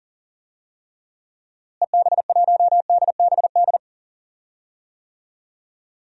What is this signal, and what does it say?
Morse code, heard as a keyed tone, about 40 wpm, 700 Hz. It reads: EB1DBD
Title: DXCC-100 Call Signs - 6 Characters - Repeated 3x - 40wpm